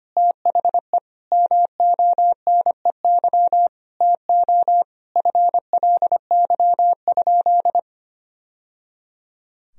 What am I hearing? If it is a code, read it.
Morse code, heard as a keyed tone, about 25 wpm, 700 Hz. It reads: THE MONEY TO FLY?
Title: Sentences from Top 1000 Words - Review Part 2 - 25wpm